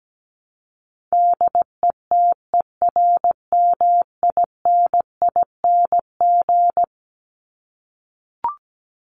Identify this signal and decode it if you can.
Morse code, heard as a keyed tone, about 17 wpm, 700 Hz. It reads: DETERMINING
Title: ING Suffix - Top 100 Words with Base Words 17wpm